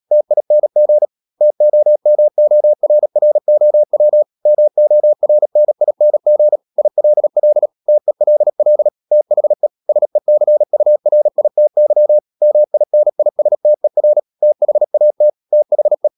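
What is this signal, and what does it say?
Morse code, heard as a keyed tone, about 37 wpm, 600 Hz. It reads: TING TOMORROW MORNING ILL TELL THE SECURITY MINISTER THAT THE
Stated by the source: Morse code practice at 35–39 wpm sending English text of Ken Follett's novel Never, from CW med Gnister.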